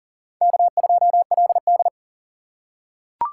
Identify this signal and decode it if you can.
Morse code, heard as a keyed tone, about 37 wpm, 700 Hz. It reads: K2LD
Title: USA Call Signs - 4 Characters - 6x Character Spacing - 40wpm